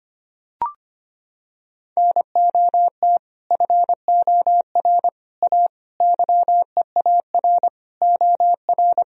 Morse code, heard as a keyed tone, about 25 wpm, 700 Hz. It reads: NOT FOR A YEAR OR
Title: Sentences from Top 1000 Words - Review Part 1 - 25wpm